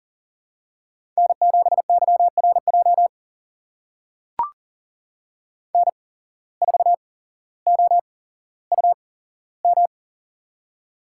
Morse code, heard as a keyed tone, about 40 wpm, spaced as for 17 wpm, 700 Hz. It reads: N7YRJ N4OUM
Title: USA Call Signs - 5 Characters - 8x Character Spacing - 40wpm